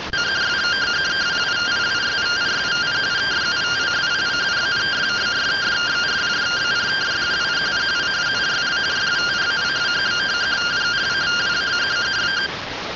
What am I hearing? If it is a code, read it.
Two-tone teleprinter signal, about 170 Hz shift.